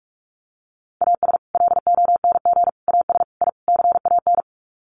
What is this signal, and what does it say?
Morse code, heard as a keyed tone, about 44 wpm, 700 Hz. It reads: AS LONG AS I CAN
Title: Sets of 5 Words Encore2 45wpm